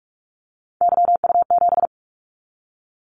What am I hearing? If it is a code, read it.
Morse code, heard as a keyed tone, about 46 wpm, 700 Hz. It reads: YU7
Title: Callsign Prefixes for Top 25 DXCC - 45wpm